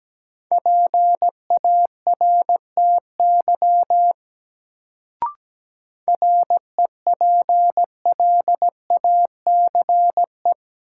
Morse code, heard as a keyed tone, about 17 wpm, 700 Hz. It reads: PARTY REPLACE
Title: Sentences from Top 1400 Words - New Word Drill - 17wpm